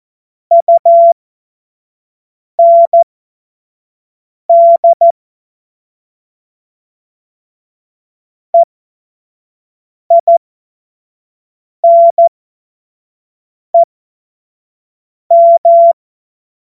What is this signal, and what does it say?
Morse code, heard as a keyed tone, about 14 wpm, spaced as for 5 wpm, 680 Hz. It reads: UND EINEM
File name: Morsekurs / Witze / Witz 06. Januar 2026 25 F70